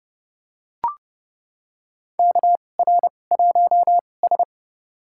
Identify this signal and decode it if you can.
Morse code, heard as a keyed tone, about 30 wpm, spaced as for 22 wpm, 700 Hz. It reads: KR1S
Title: USA Call Signs - 2x Character Spacing - 30wpm